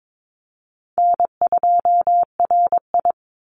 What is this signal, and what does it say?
Morse code, heard as a keyed tone, about 22 wpm, 700 Hz. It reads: N2RI
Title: Call Signs - 22wpm